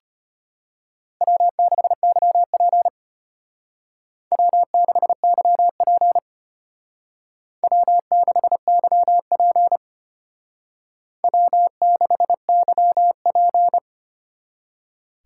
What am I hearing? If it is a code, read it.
Morse code, heard as a keyed tone, about 32 wpm, 700 Hz. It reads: W6YP W6YP W6YP W6YP